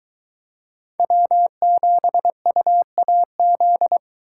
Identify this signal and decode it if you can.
Morse code, heard as a keyed tone, about 22 wpm, 700 Hz. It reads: W7UAZ